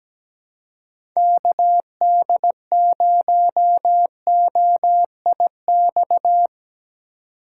Morse code, heard as a keyed tone, about 17 wpm, 700 Hz. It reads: KD0OIX
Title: K1USN SST QSO Practice - 17wpm